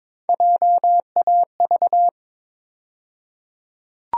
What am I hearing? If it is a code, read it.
Morse code, heard as a keyed tone, about 21 wpm, 700 Hz. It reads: JAV
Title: USA Call Sign Postfixes - 3 Letters 22wpm